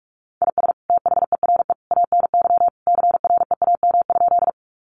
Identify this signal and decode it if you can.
Morse code, heard as a keyed tone, about 44 wpm, 700 Hz. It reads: IS THERE ANY CREAM?
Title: Sentences from Top 1500 Words 45wpm